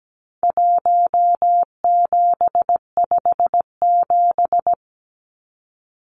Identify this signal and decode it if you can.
Morse code, heard as a keyed tone, about 17 wpm, 700 Hz. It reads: 1757
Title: Numbers - Contest Style - 4-Digit 17wpm